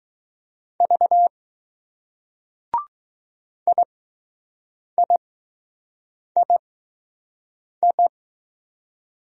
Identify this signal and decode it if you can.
Morse code, heard as a keyed tone, about 20 wpm, spaced as for 6 wpm, 700 Hz. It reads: V IIII